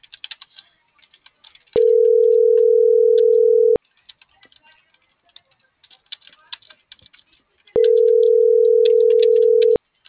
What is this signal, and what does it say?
Ringback tone (North America), 440 + 480 Hz.